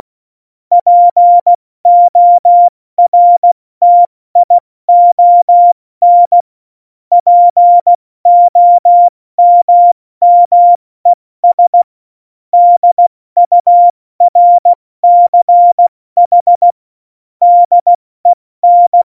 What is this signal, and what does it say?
Morse code, heard as a keyed tone, about 16 wpm, spaced as for 14 wpm, 700 Hz. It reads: PORTION POMMES DURCH DEN
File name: Morsekurs / Witze / Witz 03. Oktober 17, Tempo 70 F80